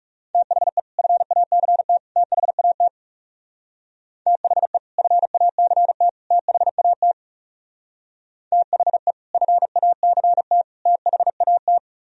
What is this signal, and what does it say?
Morse code, heard as a keyed tone, about 36 wpm, 700 Hz. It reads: THE FACT THAT THE FACT THAT THE FACT THAT
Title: Speed-Racing - Sets of 3 Words 30wpm